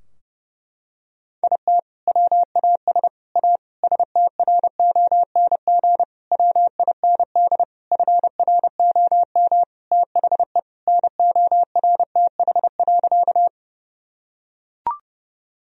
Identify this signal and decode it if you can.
Morse code, heard as a keyed tone, about 32 wpm, 700 Hz. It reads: IT WAS A STRONG WIND FROM THE NORTH.
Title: Sentences from Top 1000 Words - Review Part 2 - 30wpm